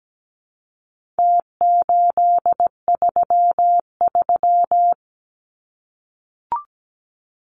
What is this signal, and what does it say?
Morse code, heard as a keyed tone, about 17 wpm, 700 Hz. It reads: T833
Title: Numbers - Contest Style - 4-Digit 17wpm